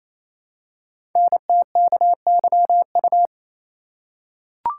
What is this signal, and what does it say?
Morse code, heard as a keyed tone, about 29 wpm, 700 Hz. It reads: NTKYU
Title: Code Groups - 5 Letters - 28wpm